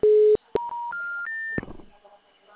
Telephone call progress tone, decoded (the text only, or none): special information tone